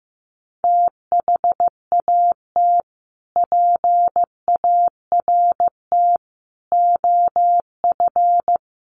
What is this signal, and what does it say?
Morse code, heard as a keyed tone, about 15 wpm, 700 Hz. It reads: THAT PART OF
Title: Sets of 5 Words Encore2 15wpm